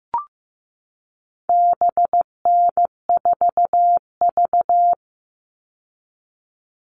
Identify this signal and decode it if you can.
Morse code, heard as a keyed tone, about 15 wpm, 700 Hz. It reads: BN4V
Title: All Cut Numbers - 4-Digit 15wpm